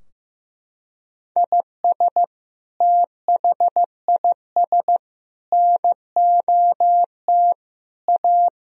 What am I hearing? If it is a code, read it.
Morse code, heard as a keyed tone, about 15 wpm, 700 Hz. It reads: IS THIS NOT A
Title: Sentences from Top 1000 Words - Review Part 2 - 15wpm